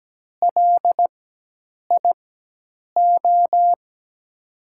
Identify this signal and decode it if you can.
Morse code, heard as a keyed tone, about 17 wpm, spaced as for 8 wpm, 700 Hz. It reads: LIO
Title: USA Call Signs - 6 Characters - 4x Character Spacing - 17wpm